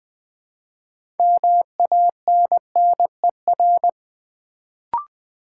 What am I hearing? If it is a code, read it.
Morse code, heard as a keyed tone, about 20 wpm, 700 Hz. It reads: MANNER
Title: Build A Word - 3 Letters At A Time - 6-Letter Words - 20wpm